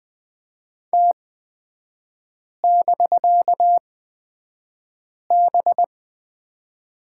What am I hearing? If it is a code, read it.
Morse code, heard as a keyed tone, about 20 wpm, spaced as for 5 wpm, 700 Hz. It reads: T<BK>B
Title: 250 Lesson 40e - BK - Three Characters 20wpm